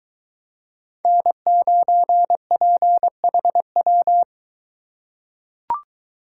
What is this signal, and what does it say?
Morse code, heard as a keyed tone, about 23 wpm, 700 Hz. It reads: N9PHW